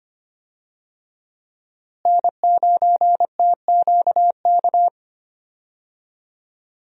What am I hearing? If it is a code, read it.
Morse code, heard as a keyed tone, about 25 wpm, 700 Hz. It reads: N9TQK
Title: USA Call Signs - 5 Characters - Repeated 3x - 25wpm